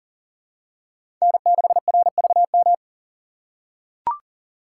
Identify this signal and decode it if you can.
Morse code, heard as a keyed tone, about 40 wpm, 700 Hz. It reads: N6RVM
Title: USA Call Signs - 2x Character Spacing - 40wpm